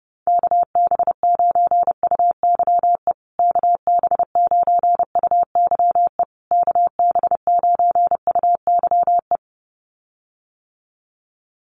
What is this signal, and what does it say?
Morse code, heard as a keyed tone, about 30 wpm, 700 Hz. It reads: KB9UYE KB9UYE KB9UYE
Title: USA Call Signs Repeated 3x 30wpm